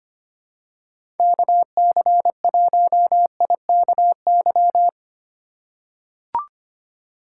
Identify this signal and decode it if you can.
Morse code, heard as a keyed tone, about 25 wpm, 700 Hz. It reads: KC1IKY
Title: USA Call Signs - 6 Characters - 3x Character Spacing - 25wpm